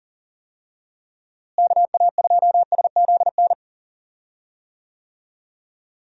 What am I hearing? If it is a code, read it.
Morse code, heard as a keyed tone, about 40 wpm, 700 Hz. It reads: KA2SZN
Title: USA Call Signs - 6 Characters - Repeated 3x - 40wpm